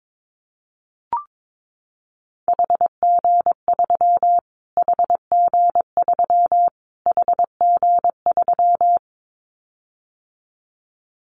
Morse code, heard as a keyed tone, about 22 wpm, 700 Hz. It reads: HG3 HG3 HG3
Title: Callsign Prefixes for Top 25 DXCC - Repeated 3x - 22wpm